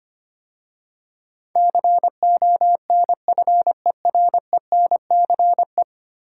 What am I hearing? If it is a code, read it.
Morse code, heard as a keyed tone, about 25 wpm, 700 Hz. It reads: CONFERENCE
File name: 10 Letter Words - First 3 Letters Given - 25wpm